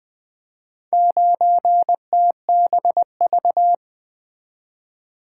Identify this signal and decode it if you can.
Morse code, heard as a keyed tone, about 20 wpm, 700 Hz. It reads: 9TBV
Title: Four Letter-Number Combinations - 20wpm